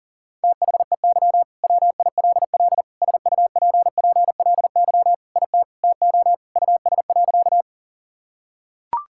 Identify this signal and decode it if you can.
Morse code, heard as a keyed tone, about 40 wpm, 700 Hz. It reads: THEY WILL SUPPLY IT TO US.
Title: Sentences from Top 1000 Words - Review Part 5 - 40wpm